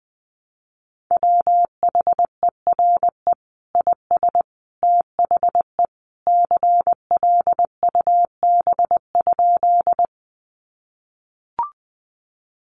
Morse code, heard as a keyed tone, about 20 wpm, 700 Hz. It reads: WHERE IS THE CLUB?
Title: Sentences from Top 1200 Words 20wpm